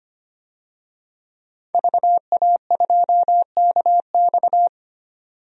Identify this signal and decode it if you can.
Morse code, heard as a keyed tone, about 25 wpm, 700 Hz. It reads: VA2KX
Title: DXCC-100 Call Signs - 5 Characters - 25wpm